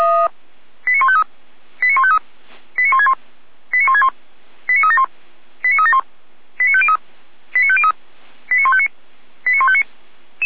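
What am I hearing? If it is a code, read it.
Five-tone selective call (ZVEI), heard as quick runs of five tones, tones of about 70 ms. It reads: 89153 89153 89161 89161 89371 89371 89503 89503 89159 89159
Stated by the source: ZVEI five-tone call received by SDR, from feuerwehr-orlamuende.org audio archive.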